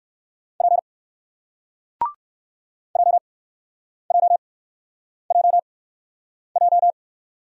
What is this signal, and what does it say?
Morse code, heard as a keyed tone, about 58 wpm, spaced as for 9 wpm, 700 Hz. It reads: P JJJJ